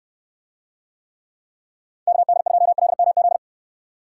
Morse code, heard as a keyed tone, about 68 wpm, 700 Hz. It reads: KD2LMZ